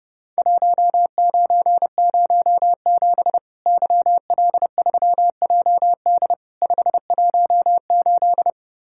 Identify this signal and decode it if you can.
Morse code, heard as a keyed tone, about 30 wpm, 700 Hz. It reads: 1907 YL3JD 518
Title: Worked All Europe DX Contest - QTC Exchange - 30wpm